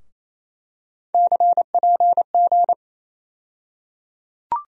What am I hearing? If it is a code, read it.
Morse code, heard as a keyed tone, about 28 wpm, 700 Hz. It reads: CPG
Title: USA Call Sign Postfixes - 3 Letters 28wpm